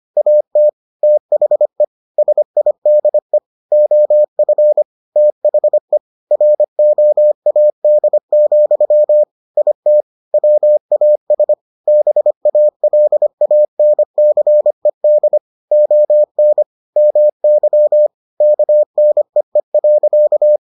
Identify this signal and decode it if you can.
Morse code, heard as a keyed tone, about 25 wpm, 600 Hz. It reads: AT THE SIDE OF THE ROAD, IT WAS BALANCED ON MY KNEE.